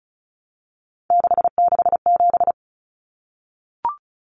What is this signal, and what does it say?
Morse code, heard as a keyed tone, about 34 wpm, 700 Hz. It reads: 667
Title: Common Cut Numbers - 3-Digit 35wpm